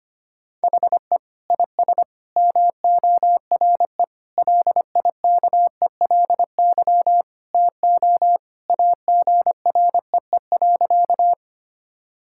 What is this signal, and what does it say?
Morse code, heard as a keyed tone, about 25 wpm, 700 Hz. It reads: HE IS MORE LIKELY TO AGREE.
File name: practice / 25wpm / Sentences with 3-Word Phrases - 25wpm